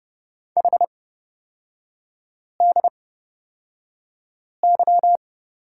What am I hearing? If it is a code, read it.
Morse code, heard as a keyed tone, about 29 wpm, spaced as for 5 wpm, 700 Hz. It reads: HDY
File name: Lesson 36e - BT - Three Characters 30wpm